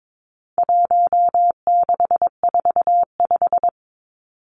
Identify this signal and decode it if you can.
Morse code, heard as a keyed tone, about 22 wpm, 700 Hz. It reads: 1645
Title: Numbers - 4-Digit 22wpm